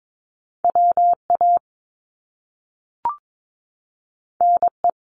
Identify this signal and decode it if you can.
Morse code, heard as a keyed tone, about 22 wpm, 700 Hz. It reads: WA NE
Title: US States 22wpm